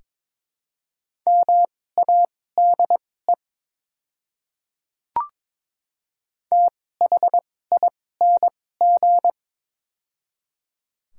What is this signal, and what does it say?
Morse code, heard as a keyed tone, about 21 wpm, spaced as for 16 wpm, 700 Hz. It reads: MADE THING
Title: 2x Character Spacing - Top 300 Words - 22wpm